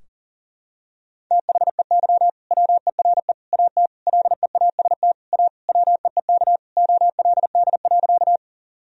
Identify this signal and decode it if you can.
Morse code, heard as a keyed tone, about 40 wpm, 700 Hz. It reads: THEY WERE AT LEAST A WEEK OLD.
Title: Sentences from Top 1000 Words - Review Part 2 - 40wpm